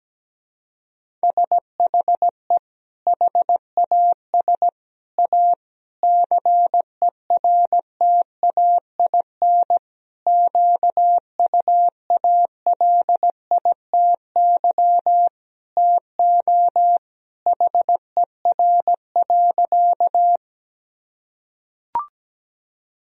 Morse code, heard as a keyed tone, about 17 wpm, 700 Hz. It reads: SHE HAS A CERTAIN QUALITY TO HER.
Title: Sentences from Top 1000 Words - Review Part 4 - 17wpm